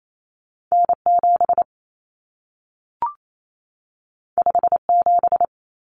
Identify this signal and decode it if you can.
Morse code, heard as a keyed tone, about 28 wpm, 700 Hz. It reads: N7 57
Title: Common Cut Numbers - 2-Digit 28wpm